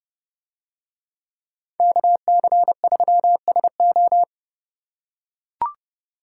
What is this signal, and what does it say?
Morse code, heard as a keyed tone, about 29 wpm, 700 Hz. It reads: KC3SO